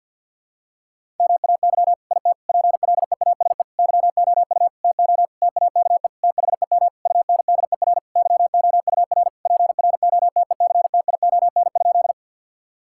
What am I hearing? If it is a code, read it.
Morse code, heard as a keyed tone, about 50 wpm, 700 Hz. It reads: MAY IT PLEASE YOU TO TAKE THEM UNDER YOUR PROTECTION?
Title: Sentences from Top 1600 Words 50wpm